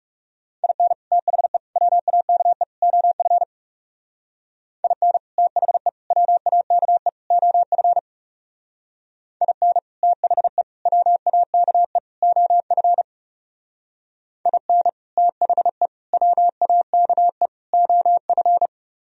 Morse code, heard as a keyed tone, about 36 wpm, 700 Hz. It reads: IN THE WAKE OF IN THE WAKE OF IN THE WAKE OF IN THE WAKE OF
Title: Speed-Racing - Sets of 4 Words 30wpm